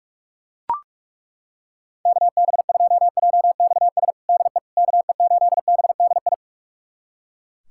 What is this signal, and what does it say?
Morse code, heard as a keyed tone, about 42 wpm, 700 Hz. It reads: KB2JXS DE KE8BDI
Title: USA Callsign DE Callsign 45wpm